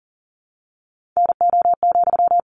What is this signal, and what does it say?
Morse code, heard as a keyed tone, about 40 wpm, 700 Hz. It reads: NO,
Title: Sentences from Top 1500 Words 40wpm